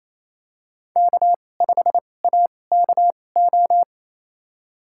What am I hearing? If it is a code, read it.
Morse code, heard as a keyed tone, about 28 wpm, spaced as for 20 wpm, 700 Hz. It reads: K5AKO
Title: USA Call Signs - 5 Characters - 2x Character Spacing - 28wpm